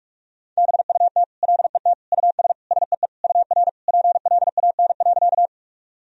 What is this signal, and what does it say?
Morse code, heard as a keyed tone, about 48 wpm, 700 Hz. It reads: BUT LET US SEE UR PLAN.
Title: Sentences from Top 500 Words and Abbreviations 45wpm